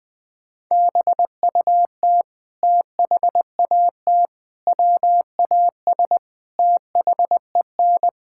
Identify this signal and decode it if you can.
Morse code, heard as a keyed tone, about 20 wpm, 700 Hz. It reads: BUT THAT WAS THEN
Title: Sets of 8 Words - 20wpm